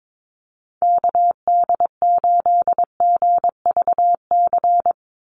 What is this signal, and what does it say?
Morse code, heard as a keyed tone, about 22 wpm, 700 Hz. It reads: KD8GVC
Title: USA Call Signs 22wpm